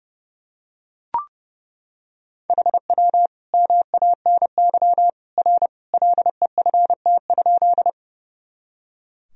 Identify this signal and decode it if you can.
Morse code, heard as a keyed tone, about 30 wpm, 700 Hz. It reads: HW MANY R LEFT?
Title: Sentences from Top 300 Words and Abbreviations 30wpm